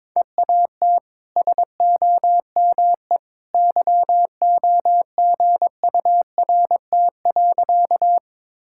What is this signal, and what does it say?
Morse code, heard as a keyed tone, about 22 wpm, 700 Hz. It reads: EAT SOME YOGURT.
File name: practice / 22wpm / Easy Sentences with Hints - 22wpm